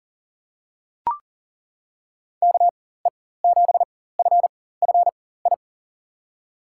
Morse code, heard as a keyed tone, about 40 wpm, spaced as for 19 wpm, 700 Hz. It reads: KE7FFI
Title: USA Call Signs - 6 Characters - 4x Character Spacing - 40wpm